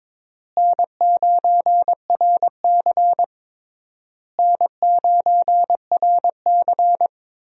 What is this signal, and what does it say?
Morse code, heard as a keyed tone, about 22 wpm, 700 Hz. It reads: N9RC N9RC
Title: USA Call Signs - 4 Characters - Repeated 3x - 22wpm